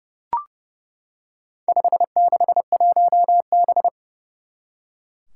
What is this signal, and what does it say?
Morse code, heard as a keyed tone, about 30 wpm, 700 Hz. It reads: 561B
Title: Four Challenging Letter-Number Combinations 30wpm